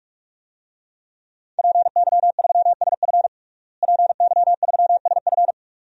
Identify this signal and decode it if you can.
Morse code, heard as a keyed tone, about 47 wpm, 700 Hz. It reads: PY3SF PY3SF
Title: DXCC-100 Call Signs - 5 Characters - Repeated 2x - 45wpm